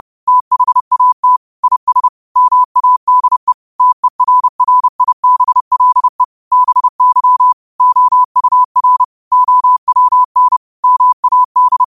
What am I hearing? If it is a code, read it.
Morse code, heard as a keyed tone, about 30 wpm, 1000 Hz. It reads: THAT IS MADE TERRIBLE BY OUR OWN MAD
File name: qotdfiles / 30wpm